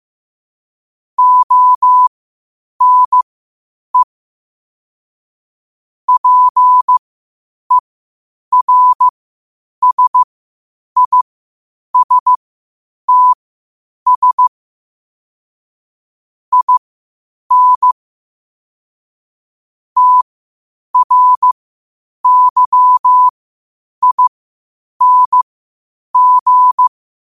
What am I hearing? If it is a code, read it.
Morse code, heard as a keyed tone, about 15 wpm, spaced as for 8 wpm, 1000 Hz. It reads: ONE PERSISTS IN TRYING
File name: qotdfiles / 8wpm